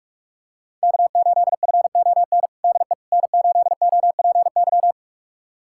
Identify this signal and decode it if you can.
Morse code, heard as a keyed tone, about 48 wpm, 700 Hz. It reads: K8FON DE N8OPY
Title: USA Callsign DE Callsign - 5 Characters - 45wpm